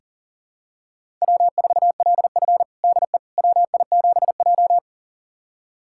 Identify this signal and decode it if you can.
Morse code, heard as a keyed tone, about 40 wpm, 700 Hz. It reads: W4LF DE WI7J
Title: USA Callsign DE Callsign - 4 Characters - 40wpm